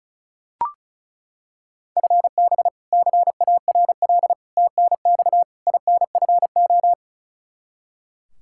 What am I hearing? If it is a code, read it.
Morse code, heard as a keyed tone, about 36 wpm, 700 Hz. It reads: FB CARL TNX INFO
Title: Practice QSO - Exchange 3 - More Info - 35wpm